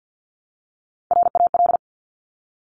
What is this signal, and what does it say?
Morse code, heard as a keyed tone, about 50 wpm, 700 Hz. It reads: RAL
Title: Top Three Letter Combinations in Words 50wpm